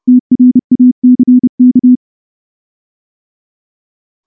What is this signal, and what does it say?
Morse code, heard as a keyed tone, about 30 wpm, 250 Hz. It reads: TRACK
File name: morse-training-30WPM